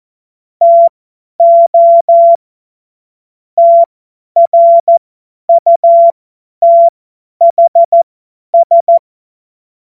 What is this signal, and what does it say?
Morse code, heard as a keyed tone, about 14 wpm, spaced as for 10 wpm, 680 Hz. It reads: TO TRUTHS
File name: Morsekurs / Witze / Witz 14. Dezember 2021, 50 F70